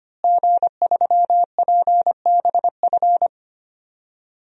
Morse code, heard as a keyed tone, about 25 wpm, 700 Hz. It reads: G3PBF
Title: DXCC-100 Call Signs - 5 Characters - Repeated 3x - 25wpm